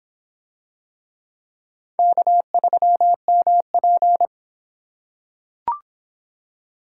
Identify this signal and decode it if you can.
Morse code, heard as a keyed tone, about 25 wpm, 700 Hz. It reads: K3MP